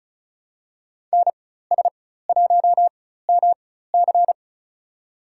Morse code, heard as a keyed tone, about 36 wpm, spaced as for 16 wpm, 700 Hz. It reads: NS1MC
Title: USA Call Signs - 5 Characters - 4x Character Spacing - 35wpm